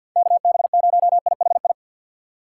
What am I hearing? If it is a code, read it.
Morse code, heard as a keyed tone, about 51 wpm, 700 Hz. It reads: KB0IHI
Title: USA Call Signs - 2x Character Spacing - 50wpm